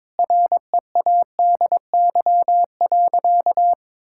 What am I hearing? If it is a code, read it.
Morse code, heard as a keyed tone, about 22 wpm, 700 Hz. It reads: READY.
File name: practice / 22wpm / Sentences with 2-Word Phrases - 22wpm